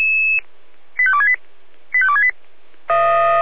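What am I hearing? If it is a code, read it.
Five-tone selective call (ZVEI), heard as quick runs of five tones, tones of about 70 ms. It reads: 85268 85268
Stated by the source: ZVEI five-tone call received by SDR, recorded by Feuerwehr Orlamünde.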